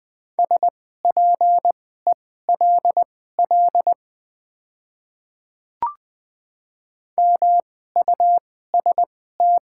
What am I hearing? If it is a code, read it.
Morse code, heard as a keyed tone, about 20 wpm, spaced as for 14 wpm, 700 Hz. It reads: SPELL MUST
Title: 2x Character Spacing - Top 200 Words - 20wpm